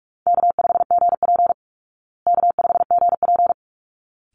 Morse code, heard as a keyed tone, about 44 wpm, 700 Hz. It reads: K5GP K5GP
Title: Call Signs Repeated 3x - 45wpm